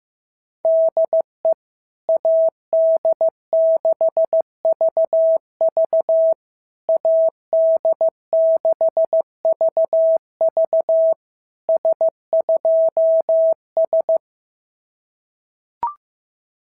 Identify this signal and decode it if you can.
Morse code, heard as a keyed tone, about 14 wpm, 650 Hz. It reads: DE AD6VV AD6VV S2S
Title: SOTA QSO Practice 15wpm